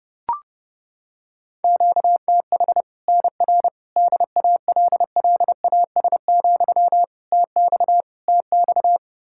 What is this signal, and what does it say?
Morse code, heard as a keyed tone, about 30 wpm, 700 Hz. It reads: QTH NR DALLAS, TX TX
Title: Practice QSO - Exchange 1 - Beginning - 30wpm